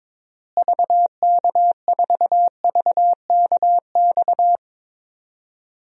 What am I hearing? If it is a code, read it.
Morse code, heard as a keyed tone, about 22 wpm, 700 Hz. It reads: VK4VKX
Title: DXCC-100 Call Signs - 6 Characters - Repeated 2x - 22wpm